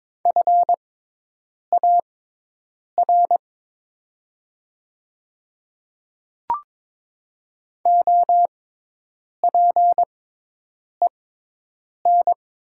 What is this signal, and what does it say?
Morse code, heard as a keyed tone, about 22 wpm, spaced as for 8 wpm, 700 Hz. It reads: FAR OPEN
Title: 6x Character Spacing - Top 300 Words - 22wpm